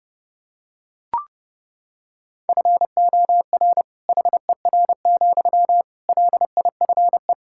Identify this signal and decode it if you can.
Morse code, heard as a keyed tone, about 32 wpm, 700 Hz. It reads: FOR HER, LIFE
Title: Sentences from Top 1600 Words 30wpm